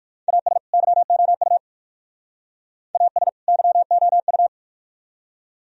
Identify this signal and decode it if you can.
Morse code, heard as a keyed tone, about 51 wpm, 700 Hz. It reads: AS YOU AS YOU
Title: Speed-Racing - Sets of 2 Words 45wpm